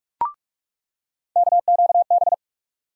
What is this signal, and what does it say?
Morse code, heard as a keyed tone, about 46 wpm, 700 Hz. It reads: KQB
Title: USA Call Sign Postfixes - 3 Letters 45wpm